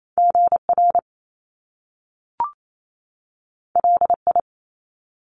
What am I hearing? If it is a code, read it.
Morse code, heard as a keyed tone, about 28 wpm, 700 Hz. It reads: GR LI
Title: Top Two Letter Combinations in Words 28wpm